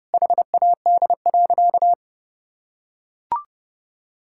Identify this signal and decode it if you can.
Morse code, heard as a keyed tone, about 32 wpm, 700 Hz. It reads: HAD.
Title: Sentences from Top 1000 Words - Review Part 6 - 30wpm